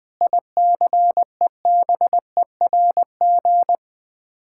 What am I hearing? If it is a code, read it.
Morse code, heard as a keyed tone, about 20 wpm, 700 Hz. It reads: ICEBERG
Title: Sets of 8 Words - 20wpm